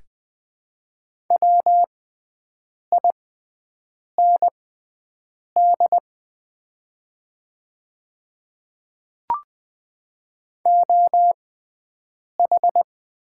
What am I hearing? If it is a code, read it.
Morse code, heard as a keyed tone, about 20 wpm, spaced as for 7 wpm, 700 Hz. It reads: WIND OH
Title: 6x Character Spacing - Top 500 Words - 20wpm